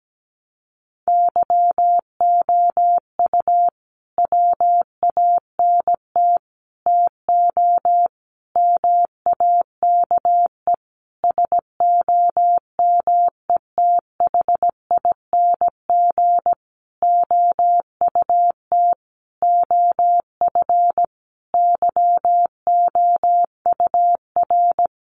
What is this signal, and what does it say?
Morse code, heard as a keyed tone, about 17 wpm, 700 Hz. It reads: YOU WANT TO MAKE SOMETHING OUT OF YOUR
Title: Sentences from Top 600 Words 17wpm